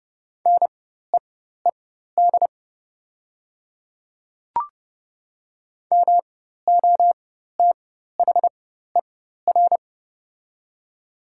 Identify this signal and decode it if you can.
Morse code, heard as a keyed tone, about 30 wpm, spaced as for 14 wpm, 700 Hz. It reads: NEED MOTHER
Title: 4x Character Spacing - Top 200 Words - 30wpm